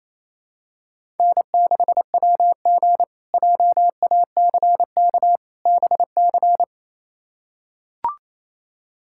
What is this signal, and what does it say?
Morse code, heard as a keyed tone, about 28 wpm, 700 Hz. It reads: N6WG JACK BC
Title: K1USN SST QSO Practice - 28wpm